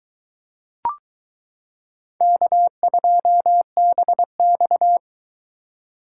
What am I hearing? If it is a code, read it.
Morse code, heard as a keyed tone, about 22 wpm, 700 Hz. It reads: K2BX